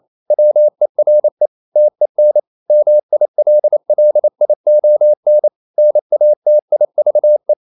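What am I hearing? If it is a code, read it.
Morse code, heard as a keyed tone, about 28 wpm, 600 Hz. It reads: WERE TEN MILLION NATIVE
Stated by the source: Morse code practice at 28 wpm, text of Ken Follett's novel Never in English, from CW med Gnister.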